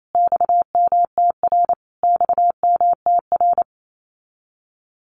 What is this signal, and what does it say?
Morse code, heard as a keyed tone, about 28 wpm, 700 Hz. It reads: XMTR XMTR
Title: Abbreviations Repeated 3x - 28wpm